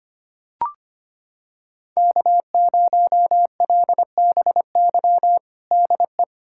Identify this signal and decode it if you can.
Morse code, heard as a keyed tone, about 25 wpm, 700 Hz. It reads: K0LBY DE
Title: USA Callsign DE Callsign - 5 Characters - 25wpm